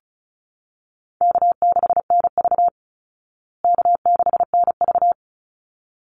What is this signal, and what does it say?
Morse code, heard as a keyed tone, about 34 wpm, 700 Hz. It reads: K6NV K6NV
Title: Easy Call Signs Repeated 2x - 35wpm